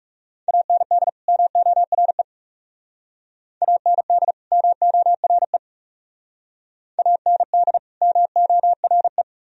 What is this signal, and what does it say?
Morse code, heard as a keyed tone, about 38 wpm, 700 Hz. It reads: AND MORE AND MORE AND MORE